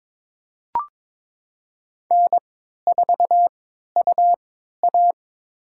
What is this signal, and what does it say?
Morse code, heard as a keyed tone, about 21 wpm, spaced as for 12 wpm, 700 Hz. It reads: N4UA
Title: USA Call Signs - 4 Characters - 3x Character Spacing - 22wpm